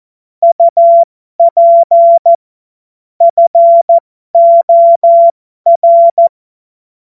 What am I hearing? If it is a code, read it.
Morse code, heard as a keyed tone, about 14 wpm, spaced as for 12 wpm, 680 Hz. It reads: UP FOR
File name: Morsekurs / Witze